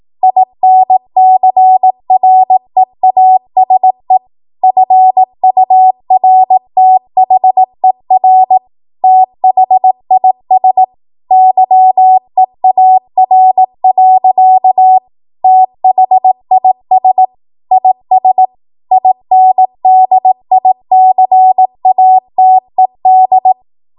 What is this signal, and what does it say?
Morse code, heard as a keyed tone, about 18 wpm, 750 Hz. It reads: INCREASE FURTHER THIS YEAR. THIS IS INDICATED
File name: Morse / 18WPM